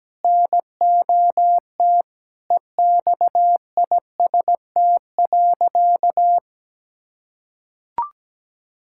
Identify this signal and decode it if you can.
Morse code, heard as a keyed tone, about 17 wpm, 700 Hz. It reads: NOT EXIST.
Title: Sentences from Top 1000 Words - Review Part 5 - 17wpm